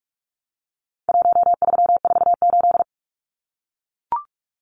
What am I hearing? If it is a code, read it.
Morse code, heard as a keyed tone, about 44 wpm, 700 Hz. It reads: 1348
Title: Numbers - Contest Style - 4-Digit 45wpm